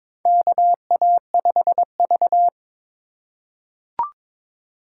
Morse code, heard as a keyed tone, about 22 wpm, 700 Hz. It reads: KA5V
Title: USA Call Signs - 3x Character Spacing - 22wpm